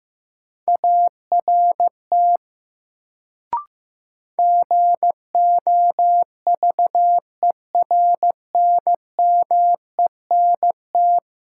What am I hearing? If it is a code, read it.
Morse code, heard as a keyed tone, about 15 wpm, 700 Hz. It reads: ART GOVERNMENT
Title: Sentences from Top 600 Words - New Word Drill - 15wpm